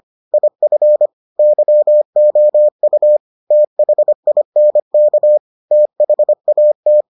Morse code, heard as a keyed tone, about 25 wpm, 600 Hz. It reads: IF YOU THINK THAT